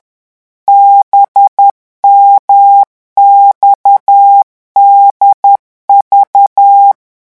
Morse code, heard as a keyed tone, about 10 wpm, 800 Hz. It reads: BMXDV